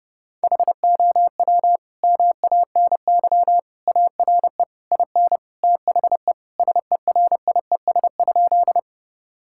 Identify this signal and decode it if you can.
Morse code, heard as a keyed tone, about 30 wpm, 700 Hz. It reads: HOW MANY ARE IN THE SERIES?
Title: Sentences from Top 1000 Words - Review Part 5 - 30wpm